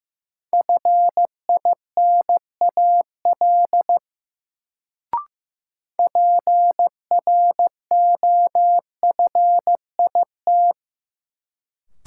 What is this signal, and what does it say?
Morse code, heard as a keyed tone, about 15 wpm, 700 Hz. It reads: FINAL PROFIT
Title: Sentences from Top 800 Words - New Word Drill - 15wpm